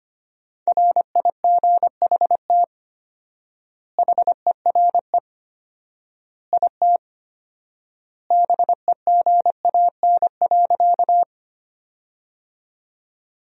Morse code, heard as a keyed tone, about 25 wpm, 700 Hz. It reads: RIGHT HERE IT BEGAN.
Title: 4x Word-Spacing - Sentences from Top 300 Words - 25wpm